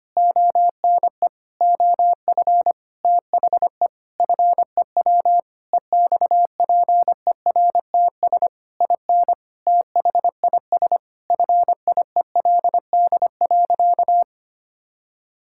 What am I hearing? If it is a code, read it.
Morse code, heard as a keyed tone, about 25 wpm, 700 Hz. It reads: ONE OF THE FEW EXPERTS IN THIS FIELD.
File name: practice / 25wpm / Sentences with 4-Word Phrases - 25wpm